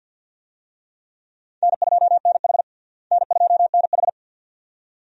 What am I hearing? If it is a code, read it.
Morse code, heard as a keyed tone, about 50 wpm, 700 Hz. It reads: N2NH N2NH
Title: USA Call Signs - 4 Characters - Repeated 2x - 50wpm